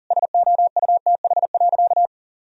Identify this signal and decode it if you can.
Morse code, heard as a keyed tone, about 37 wpm, 700 Hz. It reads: SOUTH.
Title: Sentences from Top 1000 Words - Review Part 2 - 40wpm